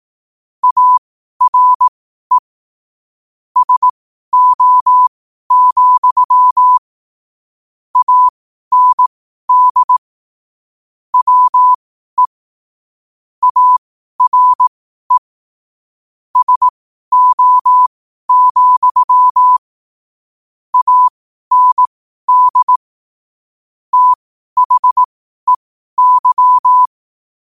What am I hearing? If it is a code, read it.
Morse code, heard as a keyed tone, about 18 wpm, spaced as for 12 wpm, 1000 Hz. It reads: ARE SO, AND WE ARE SO, AND THEY
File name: qotdfiles / 12wpm